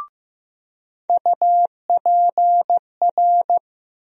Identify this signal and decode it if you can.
Morse code, heard as a keyed tone, about 15 wpm, 700 Hz. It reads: UPR
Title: USA Call Sign Postfixes - 3 Letters 15wpm